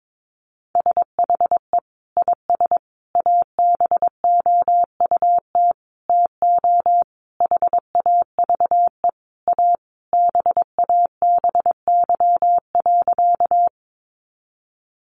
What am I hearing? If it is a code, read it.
Morse code, heard as a keyed tone, about 22 wpm, 700 Hz. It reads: SHE IS ABOUT TO HAVE A BABY.
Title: Sentences from Top 1300 Words 22wpm